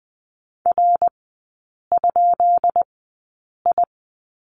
Morse code, heard as a keyed tone, about 20 wpm, spaced as for 8 wpm, 700 Hz. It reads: R?I
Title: 191 Lesson 31e - Letter v - Three Characters 20wpm